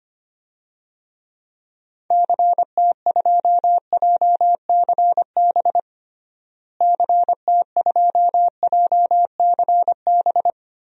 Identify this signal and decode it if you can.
Morse code, heard as a keyed tone, about 25 wpm, 700 Hz. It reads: CT2JCB CT2JCB
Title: DXCC-100 Call Signs - 6 Characters - Repeated 2x - 25wpm